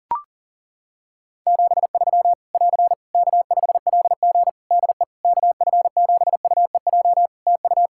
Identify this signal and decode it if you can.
Morse code, heard as a keyed tone, about 40 wpm, 700 Hz. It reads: 73 <AR> K5LG DE KF7UEJ TU
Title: K1USN SST QSO Practice 40wpm